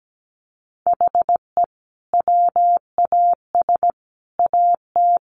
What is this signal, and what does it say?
Morse code, heard as a keyed tone, about 17 wpm, 700 Hz. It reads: HE WAS AT
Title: Sentences from Top 200 Words 17wpm